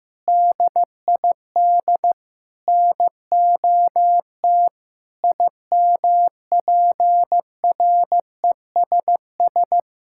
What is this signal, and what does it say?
Morse code, heard as a keyed tone, about 15 wpm, 700 Hz. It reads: DID NOT IMPRESS
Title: Sentences from Top 1600 Words 15wpm